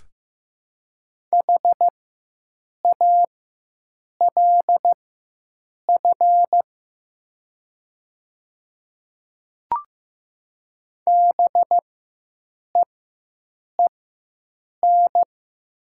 Morse code, heard as a keyed tone, about 15 wpm, spaced as for 7 wpm, 700 Hz. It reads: HALF BEEN
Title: 4x Character Spacing - Top 500 Words - 15wpm